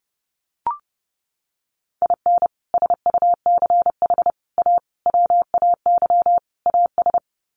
Morse code, heard as a keyed tone, about 30 wpm, 700 Hz. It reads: IN SUCH A WAY AS
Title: Sets of 5 Words Encore 30wpm